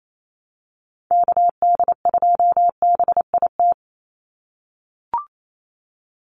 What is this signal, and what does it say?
Morse code, heard as a keyed tone, about 28 wpm, 700 Hz. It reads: KD2BIT
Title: USA Call Signs 28wpm